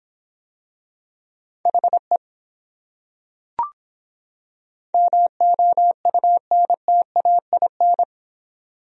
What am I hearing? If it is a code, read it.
Morse code, heard as a keyed tone, about 25 wpm, 700 Hz. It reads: HE MOUNTAIN